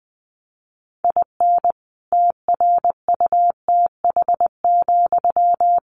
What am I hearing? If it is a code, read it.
Morse code, heard as a keyed tone, about 20 wpm, 700 Hz. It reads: IN TRUTH,
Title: Sentences from Top 1300 Words 20wpm